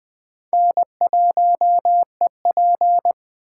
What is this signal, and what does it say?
Morse code, heard as a keyed tone, about 21 wpm, 700 Hz. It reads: N1EP